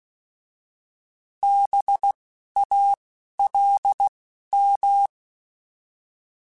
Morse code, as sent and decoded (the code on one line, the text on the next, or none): -... / .- / .-.. / --
B A L M